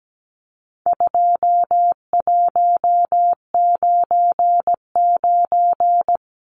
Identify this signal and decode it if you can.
Morse code, heard as a keyed tone, about 17 wpm, 700 Hz. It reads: 2199
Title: Numbers - 4-Digit 17wpm